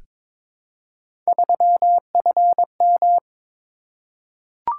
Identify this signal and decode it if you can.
Morse code, heard as a keyed tone, about 22 wpm, 700 Hz. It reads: 3FM
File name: practice / 22wpm / Three Letter-Number Combinations - 22wpm